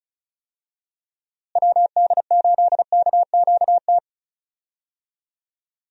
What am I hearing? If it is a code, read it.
Morse code, heard as a keyed tone, about 34 wpm, 700 Hz. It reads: WD8KQT
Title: USA Call Signs - 6 Characters - Repeated 3x - 35wpm